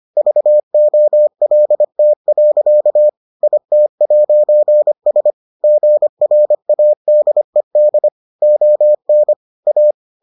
Morse code, heard as a keyed tone, about 25 wpm, 600 Hz. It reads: VOLT. IT'S GRADED ON A